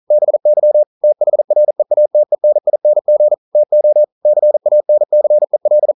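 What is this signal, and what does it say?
Morse code, heard as a keyed tone, about 40 wpm, 600 Hz. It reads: BY THREATENING TO CANCEL
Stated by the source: Morse code practice at 41 wpm, from CW med Gnister.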